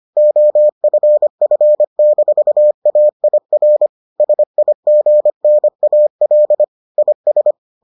Morse code, heard as a keyed tone, about 25 wpm, 600 Hz. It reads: OFF-AIR SIGNAL IS